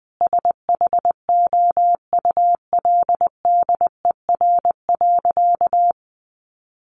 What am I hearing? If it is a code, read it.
Morse code, heard as a keyed tone, about 20 wpm, 700 Hz. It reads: SHOULDER.
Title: Additional Easy Phrases 20wpm